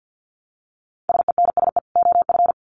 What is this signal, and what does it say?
Morse code, heard as a keyed tone, about 50 wpm, 700 Hz. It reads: SENSE OF
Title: Sets of 2 Words 50wpm